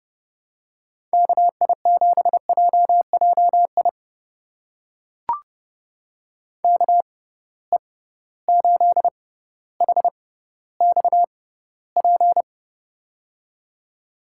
Morse code, heard as a keyed tone, about 32 wpm, spaced as for 16 wpm, 700 Hz. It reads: KI7JJI KE8HXP
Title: USA Call Signs - 6 Characters - 6x Character Spacing - 30wpm